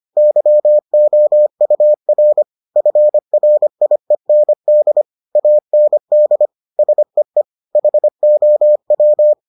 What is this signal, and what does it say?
Morse code, heard as a keyed tone, about 25 wpm, 600 Hz. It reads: YOUR FRIEND AND SEE HOW